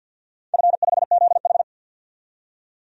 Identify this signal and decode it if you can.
Morse code, heard as a keyed tone, about 50 wpm, 700 Hz. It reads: F5ZH
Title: DXCC-100 Call Signs - 4 Characters - 50wpm